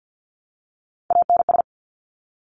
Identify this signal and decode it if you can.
Morse code, heard as a keyed tone, about 50 wpm, 700 Hz. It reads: ANS
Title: Top Three Letter Combinations in Words 50wpm